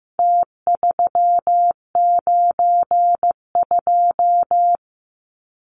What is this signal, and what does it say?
Morse code, heard as a keyed tone, about 15 wpm, 700 Hz. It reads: T392
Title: Numbers - Contest Style - 4-Digit 15wpm